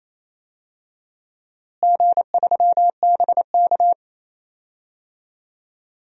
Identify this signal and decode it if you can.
Morse code, heard as a keyed tone, about 28 wpm, 700 Hz. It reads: G3BK
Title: DXCC-100 Call Signs - 4 Characters - Repeated 3x - 28wpm